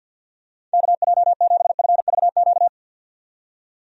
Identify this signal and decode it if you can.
Morse code, heard as a keyed tone, about 50 wpm, 700 Hz. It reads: KJ7FVQ
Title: USA Call Signs - 6 Characters - 50wpm